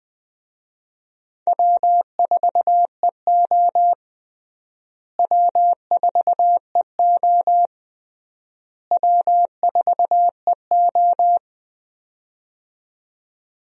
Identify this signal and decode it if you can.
Morse code, heard as a keyed tone, about 20 wpm, 700 Hz. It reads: W4EO W4EO W4EO
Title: USA Call Signs - 4 Characters - Repeated 3x - 20wpm